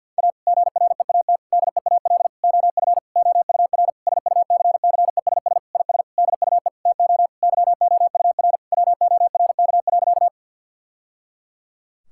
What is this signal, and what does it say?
Morse code, heard as a keyed tone, about 50 wpm, 700 Hz. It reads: A GREAT DEAL OF OUR SUCCESS IS DUE TO YOUR WORK.
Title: Sentences from Top 1000 Words - Review Part 6 - 50wpm